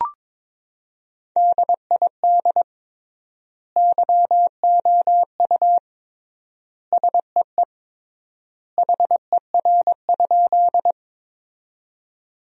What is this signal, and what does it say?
Morse code, heard as a keyed tone, about 22 wpm, 700 Hz. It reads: DID YOU SEE HER?
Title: 3x Word-Spacing - Sentences from Top 100 Words - 22wpm